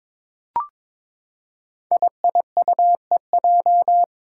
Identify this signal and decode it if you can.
Morse code, heard as a keyed tone, about 23 wpm, 700 Hz. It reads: IIUEJ
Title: Five Challenging Letter-Number Combinations 22wpm